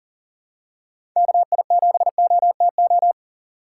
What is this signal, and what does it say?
Morse code, heard as a keyed tone, about 40 wpm, 700 Hz. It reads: KI7OTO
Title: USA Call Signs - 6 Characters - 6x Character Spacing - 40wpm